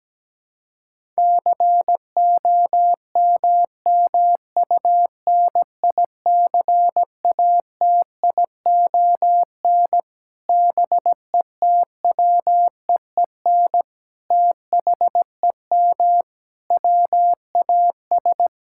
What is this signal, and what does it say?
Morse code, heard as a keyed tone, about 17 wpm, 700 Hz. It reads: COMMUNICATION BETWEEN THEM WAS
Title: Sentences from Top 1000 Words - Review Part 6 - 17wpm